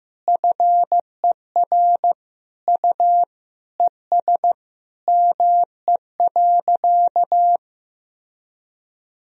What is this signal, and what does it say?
Morse code, heard as a keyed tone, about 15 wpm, 700 Hz. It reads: FER U ES ME.
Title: Sentences from Top 500 Words and Abbreviations 15wpm